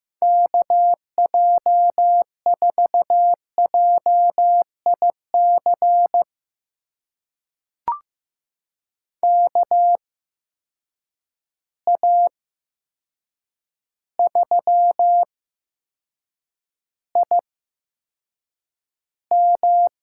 Morse code, heard as a keyed tone, about 15 wpm, spaced as for 7 wpm, 700 Hz. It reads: KJ4JIC KA3IM
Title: USA Call Signs - 6 Characters - 8x Character Spacing - 15wpm